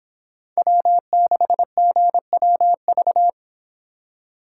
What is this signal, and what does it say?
Morse code, heard as a keyed tone, about 25 wpm, 700 Hz. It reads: W6GWV